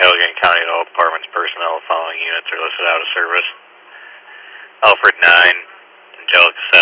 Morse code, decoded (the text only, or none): none